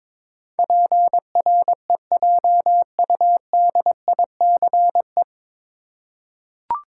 Morse code, heard as a keyed tone, about 22 wpm, 700 Hz. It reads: PREJUDICE
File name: 9 Letter Words - First 3 Letters Given - 22wpm